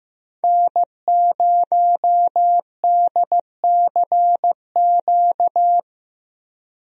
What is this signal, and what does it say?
Morse code, heard as a keyed tone, about 15 wpm, 700 Hz. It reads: N0DCQ
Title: USA Call Signs - 5 Characters - 15wpm